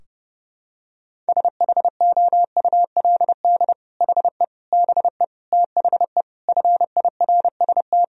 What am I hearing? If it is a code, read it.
Morse code, heard as a keyed tone, about 30 wpm, 700 Hz. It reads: SHOULD HE BE THE FIRST
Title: Sentences from Top 1000 Words - Review Part 4 - 30wpm